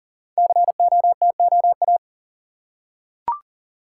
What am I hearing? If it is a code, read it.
Morse code, heard as a keyed tone, about 38 wpm, 700 Hz. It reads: COTOA
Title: Code Groups - 5 Letters - 40wpm